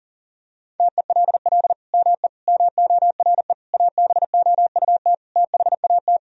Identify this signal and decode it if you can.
Morse code, heard as a keyed tone, about 40 wpm, 700 Hz. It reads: TELL ME MORE ABOUT THAT
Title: Sentences from Top 1000 Words - Review Part 4 - 40wpm